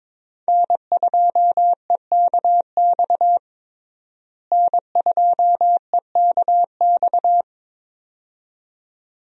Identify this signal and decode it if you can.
Morse code, heard as a keyed tone, about 21 wpm, 700 Hz. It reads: N2EKX N2EKX
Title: USA Call Signs - 5 Characters - Repeated 2x - 22wpm